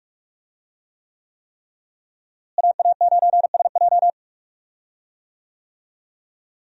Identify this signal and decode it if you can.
Morse code, heard as a keyed tone, about 46 wpm, 700 Hz. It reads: AA9SJ
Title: USA Call Signs - 5 Characters - Repeated 3x - 45wpm